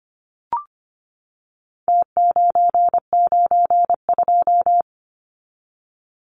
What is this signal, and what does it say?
Morse code, heard as a keyed tone, about 25 wpm, 700 Hz. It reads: T992
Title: Numbers - Contest Style - 4-Digit 25wpm